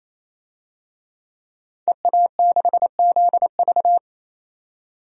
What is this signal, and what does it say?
Morse code, heard as a keyed tone, about 28 wpm, 700 Hz. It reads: EA6ZV
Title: DXCC-100 Call Signs - 5 Characters - 28wpm